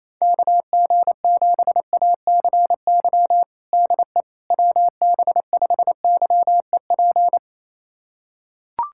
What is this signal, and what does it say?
Morse code, heard as a keyed tone, about 28 wpm, 700 Hz. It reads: KG7ACY DE WB5YEP
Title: USA Callsign DE Callsign - 6 Characters - 28wpm